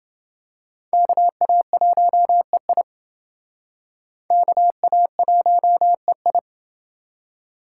Morse code, heard as a keyed tone, about 29 wpm, 700 Hz. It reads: KA1EI KA1EI